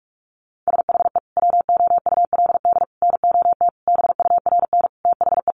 Morse code, heard as a keyed tone, about 44 wpm, 700 Hz. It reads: SHE WOULD NOT BURN THE
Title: Sentences from Top 1300 Words 45wpm